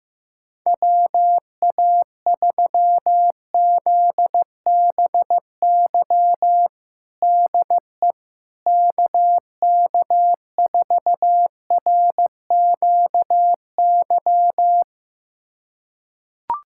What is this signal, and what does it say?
Morse code, heard as a keyed tone, about 16 wpm, 700 Hz. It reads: WA3ZBY DE KK4RQY
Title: USA Callsign DE Callsign 15wpm